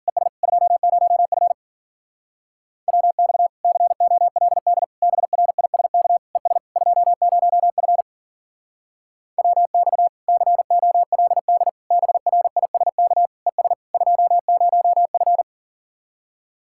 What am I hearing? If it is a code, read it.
Morse code, heard as a keyed tone, about 50 wpm, 700 Hz. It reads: ES 20F WX COLD BRISK ES 20F WX COLD BRISK ES 20F